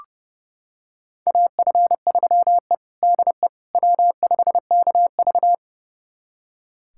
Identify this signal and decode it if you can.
Morse code, heard as a keyed tone, about 30 wpm, 700 Hz. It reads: AF3E DE W5KV
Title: USA Callsign DE Callsign - 4 Characters - 30wpm